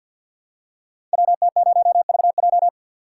Morse code, heard as a keyed tone, about 47 wpm, 700 Hz. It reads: WT0VJ
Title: Code Groups - 5 Letter:Numbers - 50wpm